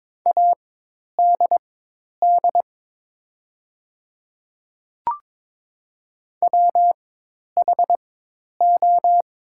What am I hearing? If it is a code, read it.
Morse code, heard as a keyed tone, about 22 wpm, spaced as for 10 wpm, 700 Hz. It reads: ADD WHO
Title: 4x Character Spacing - Top 300 Words - 22wpm